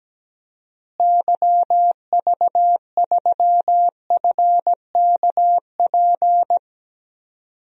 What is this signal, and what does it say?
Morse code, heard as a keyed tone, about 17 wpm, 700 Hz. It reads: YV3FKP
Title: DXCC-100 Call Signs - 6 Characters - Repeated 2x - 17wpm